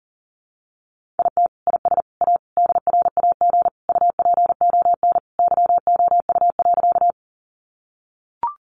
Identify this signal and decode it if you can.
Morse code, heard as a keyed tone, about 40 wpm, 700 Hz. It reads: IT IS A DRAG UPON YOU.
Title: Sentences from Top 1500 Words 40wpm